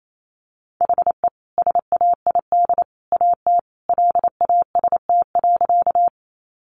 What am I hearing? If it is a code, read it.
Morse code, heard as a keyed tone, about 28 wpm, 700 Hz. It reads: HE SAID AT LAST.
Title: Sentences from Top 300 Words 28wpm